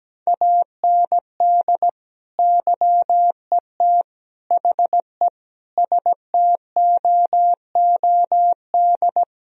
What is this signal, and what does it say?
Morse code, heard as a keyed tone, about 17 wpm, 700 Hz. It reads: AND YET HE STOOD
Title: Sentences from Top 1000 Words - Review Part 3 - 17wpm